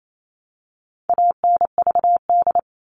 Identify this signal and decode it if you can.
Morse code, heard as a keyed tone, about 28 wpm, 700 Hz. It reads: ANVD
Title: All Cut Numbers - 4-Digit 28wpm